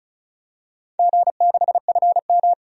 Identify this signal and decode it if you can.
Morse code, heard as a keyed tone, about 36 wpm, 700 Hz. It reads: G6FM
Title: DXCC-100 Call Signs - 4 Characters - 35wpm